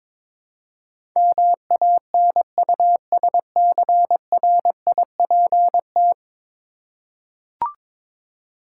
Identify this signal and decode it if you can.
Morse code, heard as a keyed tone, about 22 wpm, 700 Hz. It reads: MANUSCRIPT
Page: morse-code-ninja-practice-22wpm